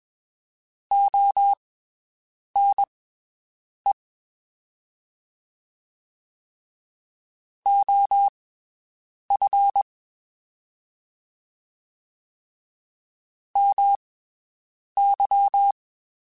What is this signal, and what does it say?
Morse code, heard as a keyed tone, about 21 wpm, spaced as for 7 wpm, 800 Hz. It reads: ONE OF MY
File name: (spaced) / QSOs Spaced-mp3s / WD0FBT 1 @20 sp